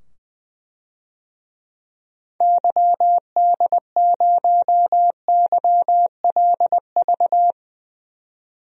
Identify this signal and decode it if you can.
Morse code, heard as a keyed tone, about 20 wpm, 700 Hz. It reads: YD0YLV
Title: DXCC-100 Call Signs - 6 Characters - 20wpm